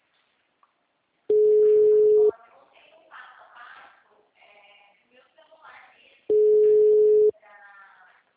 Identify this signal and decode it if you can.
Ringback tone (Europe, 425 Hz), with speech.